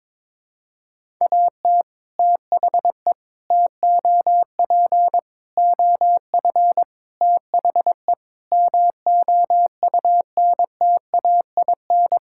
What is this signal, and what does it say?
Morse code, heard as a keyed tone, about 22 wpm, 700 Hz. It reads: AT THE TOP OF THE MOUNTAIN
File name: practice / 22wpm / Sentences with 3-Word Phrases - 22wpm